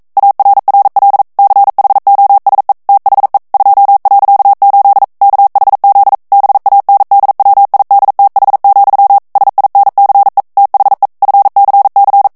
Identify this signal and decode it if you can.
Morse code, heard as a keyed tone, about 42 wpm, 800 Hz. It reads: ARRL CHOSE THE 2.8 KHZ BANDWIDTH, SINCE THE FCC